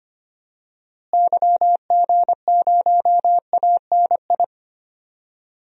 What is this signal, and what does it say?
Morse code, heard as a keyed tone, about 25 wpm, 700 Hz. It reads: YG0ANI
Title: DXCC-100 Call Signs - 6 Characters - Repeated 3x - 25wpm